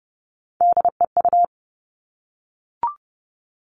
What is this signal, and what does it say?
Morse code, heard as a keyed tone, about 30 wpm, 700 Hz. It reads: DEU
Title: Three Challenging Letter-Number Combinations 30wpm